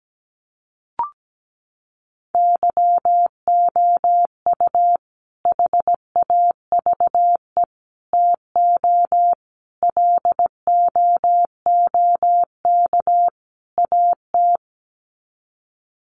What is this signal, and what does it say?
Morse code, heard as a keyed tone, about 17 wpm, 700 Hz. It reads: YOU HAVE TO LOOK AT
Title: Sets of 5 Words Encore 17wpm